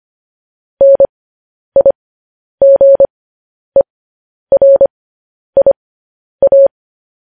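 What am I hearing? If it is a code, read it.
Morse code, heard as a keyed tone, about 25 wpm, spaced as for 10 wpm, 550 Hz. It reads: NIGERIA
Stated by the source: Morse code news headlines practice at 10 wpm.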